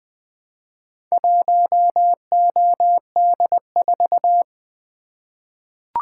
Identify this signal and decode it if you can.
Morse code, heard as a keyed tone, about 20 wpm, 700 Hz. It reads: 1OD4
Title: Four Letter-Number Combinations - 20wpm